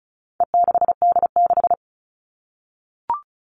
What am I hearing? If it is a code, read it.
Morse code, heard as a keyed tone, about 34 wpm, 700 Hz. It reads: E6D6
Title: All Cut Numbers - 4-Digit 35wpm